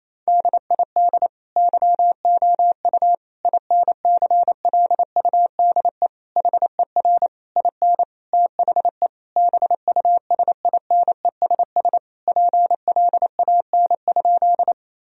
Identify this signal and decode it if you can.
Morse code, heard as a keyed tone, about 28 wpm, 700 Hz. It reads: DID YOU INCLUDE HER IN THE BUSINESS PLAN?
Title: Sentences from Top 1000 Words - Review Part 3 - 28wpm